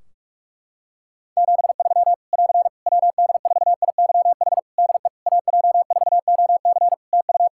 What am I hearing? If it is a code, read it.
Morse code, heard as a keyed tone, about 44 wpm, 700 Hz. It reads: 73 <AR> WD4IYH DE AJ4OC TU
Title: Practice QSO - Exchange 4 - Ending - 45wpm